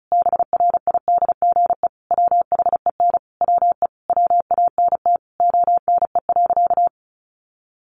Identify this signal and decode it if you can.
Morse code, heard as a keyed tone, about 34 wpm, 700 Hz. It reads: BRIDGE WHEN WE WANT ONE.
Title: Sentences from Top 1200 Words 35wpm